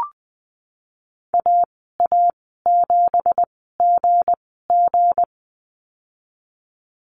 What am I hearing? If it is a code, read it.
Morse code, heard as a keyed tone, about 19 wpm, spaced as for 14 wpm, 700 Hz. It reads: AA7GG
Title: 193 Lesson 31g - Letter v - Call Signs 20wpm